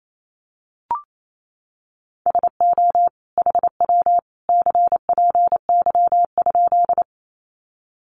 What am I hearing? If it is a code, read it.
Morse code, heard as a keyed tone, about 28 wpm, 700 Hz. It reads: SO HW CPY?
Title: Common QSO Phrases 28wpm